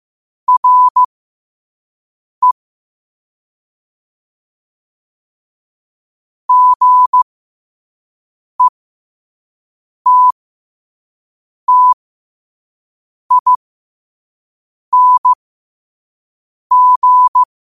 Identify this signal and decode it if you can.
Morse code, heard as a keyed tone, about 15 wpm, spaced as for 5 wpm, 1000 Hz. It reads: RE GETTING
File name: qotdfiles / 5wpm